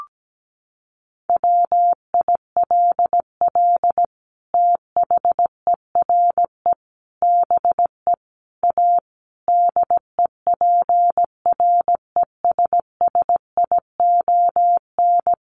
Morse code, heard as a keyed tone, about 17 wpm, 700 Hz. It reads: WILL THERE BE A DEPRESSION
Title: Sentences from Top 1300 Words 17wpm